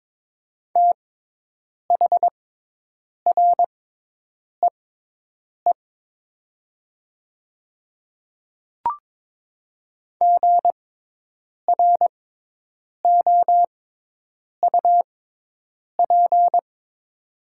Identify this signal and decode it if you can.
Morse code, heard as a keyed tone, about 22 wpm, spaced as for 8 wpm, 700 Hz. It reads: THREE GROUP
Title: 6x Character Spacing - Top 300 Words - 22wpm